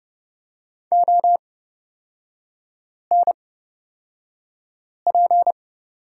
Morse code, heard as a keyed tone, about 30 wpm, spaced as for 5 wpm, 700 Hz. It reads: ONP
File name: Lesson 36e - BT - Three Characters 30wpm